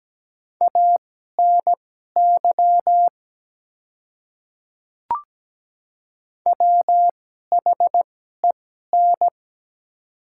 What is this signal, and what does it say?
Morse code, heard as a keyed tone, about 17 wpm, spaced as for 12 wpm, 700 Hz. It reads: ANY WHEN
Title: 2x Character Spacing - Top 200 Words - 17wpm